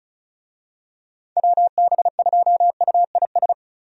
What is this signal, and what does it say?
Morse code, heard as a keyed tone, about 36 wpm, 700 Hz. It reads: WB2UIS
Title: USA Call Signs - 6 Characters - Repeated 2x - 35wpm